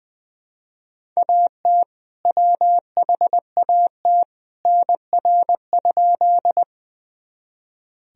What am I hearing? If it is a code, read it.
Morse code, heard as a keyed tone, about 21 wpm, 700 Hz. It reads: AT WHAT NR?
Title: Sentences from Top 100 Words and Abbreviations 20wpm